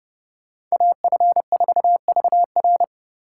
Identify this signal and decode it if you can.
Morse code, heard as a keyed tone, about 39 wpm, 700 Hz. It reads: AF4VR